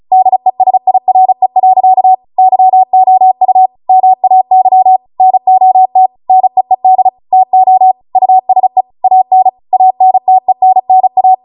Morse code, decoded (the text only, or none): DESIRE. YOU MAY NOT NEED TO USE AN ANTENNA